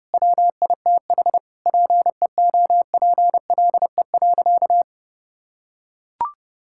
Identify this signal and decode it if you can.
Morse code, heard as a keyed tone, about 32 wpm, 700 Hz. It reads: WITH PEOPLE.
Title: Sentences from Top 1000 Words - Review Part 5 - 30wpm